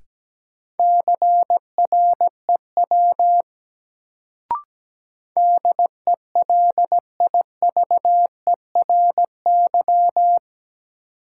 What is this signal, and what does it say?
Morse code, heard as a keyed tone, about 17 wpm, 700 Hz. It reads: CREW DELIVERY
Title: Sentences from Top 1500 Words - New Word Drill Alt - 17wpm